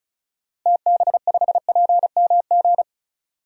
Morse code, heard as a keyed tone, about 36 wpm, 700 Hz. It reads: TB5PMG
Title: DXCC-100 Call Signs - 6 Characters - Repeated 3x - 35wpm